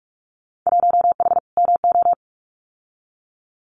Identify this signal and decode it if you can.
Morse code, heard as a keyed tone, about 46 wpm, 700 Hz. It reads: 1H MO
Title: Winter Field Day Exchanges 45wpm